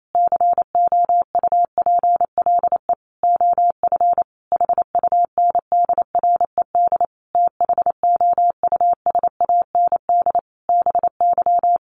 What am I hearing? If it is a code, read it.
Morse code, heard as a keyed tone, about 28 wpm, 700 Hz. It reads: COUPLE OF HUNDRED THOUSAND BY
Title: Sentences from Top 1100 Words 28wpm